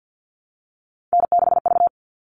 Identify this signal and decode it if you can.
Morse code, heard as a keyed tone, about 50 wpm, 700 Hz. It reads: N6V
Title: All Cut Numbers - 3-Digit 50wpm